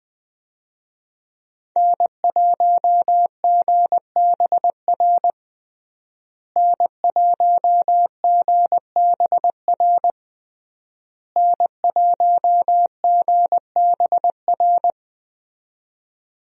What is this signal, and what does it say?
Morse code, heard as a keyed tone, about 20 wpm, 700 Hz. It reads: N1GBR N1GBR N1GBR
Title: USA Call Signs - 5 Characters - Repeated 3x - 20wpm